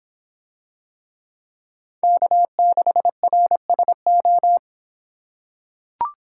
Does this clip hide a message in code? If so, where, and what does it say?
Yes: Morse code from 2.0 to 4.6 s, heard as a keyed tone, about 25 wpm, 700 Hz. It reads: K6RSO